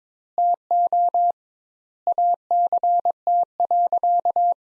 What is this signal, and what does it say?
Morse code, heard as a keyed tone, about 21 wpm, 700 Hz. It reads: TO ACT.
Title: Sentences Top 200 Words - 22wpm